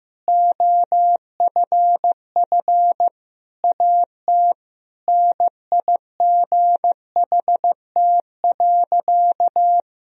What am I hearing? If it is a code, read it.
Morse code, heard as a keyed tone, about 15 wpm, 700 Hz. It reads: OFF AT NIGHT.
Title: Sentences from Top 1000 Words - Review Part 3 - 15wpm